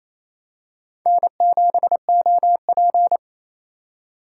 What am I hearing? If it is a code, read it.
Morse code, heard as a keyed tone, about 28 wpm, 700 Hz. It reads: N7OP
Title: USA Call Signs - 4 Characters - 2x Character Spacing - 28wpm